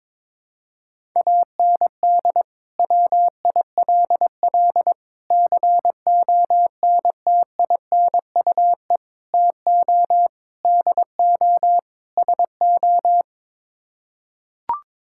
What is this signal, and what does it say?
Morse code, heard as a keyed tone, about 22 wpm, 700 Hz. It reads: AND WILL CONTINUE TO DO SO
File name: practice / 22wpm / Sets of 6 Words - 22wpm